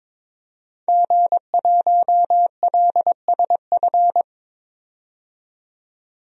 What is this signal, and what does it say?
Morse code, heard as a keyed tone, about 22 wpm, 700 Hz. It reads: G1LSF
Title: DXCC-100 Call Signs - 5 Characters - 22wpm